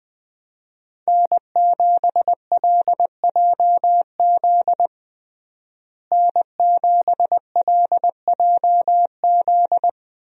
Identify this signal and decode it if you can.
Morse code, heard as a keyed tone, about 21 wpm, 700 Hz. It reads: N7LJZ N7LJZ
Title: USA Call Signs - 5 Characters - Repeated 2x - 20wpm